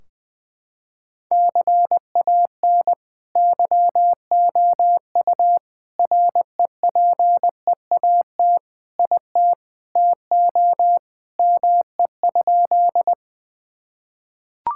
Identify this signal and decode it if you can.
Morse code, heard as a keyed tone, about 20 wpm, 700 Hz. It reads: CAN YOU REPEAT IT TO ME?
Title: Sentences from Top 1600 Words 20wpm